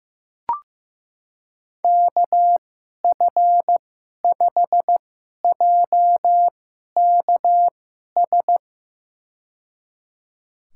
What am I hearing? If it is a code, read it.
Morse code, heard as a keyed tone, about 15 wpm, spaced as for 11 wpm, 700 Hz. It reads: KF5JKS
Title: USA Call Signs - 2x Character Spacing - 15wpm